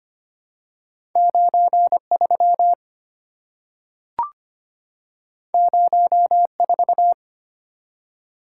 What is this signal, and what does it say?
Morse code, heard as a keyed tone, about 25 wpm, 700 Hz. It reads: 93 04
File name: Two Letter-Number Combinations - 25wpm